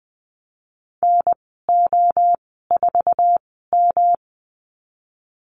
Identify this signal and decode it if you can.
Morse code, heard as a keyed tone, about 20 wpm, spaced as for 15 wpm, 700 Hz. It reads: NO4M
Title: 147 Lesson 24g - Letter y - Call Signs 20wpm